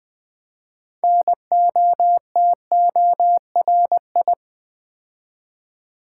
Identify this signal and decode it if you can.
Morse code, heard as a keyed tone, about 20 wpm, 700 Hz. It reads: NOTORI
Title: Build A Word - 3 Letters At A Time - 9-Letter Words - 20wpm